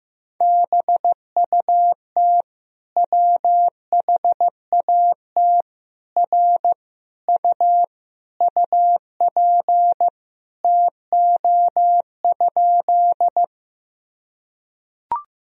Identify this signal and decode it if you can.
Morse code, heard as a keyed tone, about 15 wpm, 700 Hz. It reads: BUT WHAT R U UP TO?
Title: Sentences from Top 100 Words and Abbreviations 15wpm